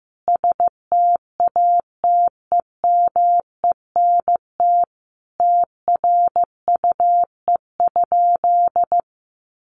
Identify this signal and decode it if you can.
Morse code, heard as a keyed tone, about 15 wpm, 700 Hz. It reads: STATEMENT TRUE?
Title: Sentences from Top 1200 Words 15wpm